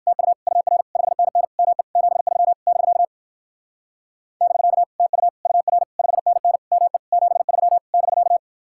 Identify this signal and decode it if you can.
Morse code, heard as a keyed tone, about 49 wpm, 700 Hz. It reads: TU UR 5NN ME 73 <BK> <BK> TU UR 5NN ME 73 <BK>
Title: Speed-Racing Common QSO Phrases - 45wpm